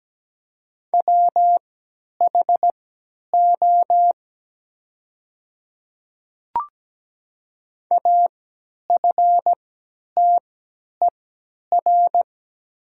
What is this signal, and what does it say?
Morse code, heard as a keyed tone, about 17 wpm, spaced as for 10 wpm, 700 Hz. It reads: WHO AFTER
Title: 3x Character Spacing - Top 200 Words - 17wpm